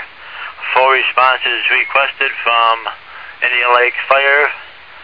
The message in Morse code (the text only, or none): none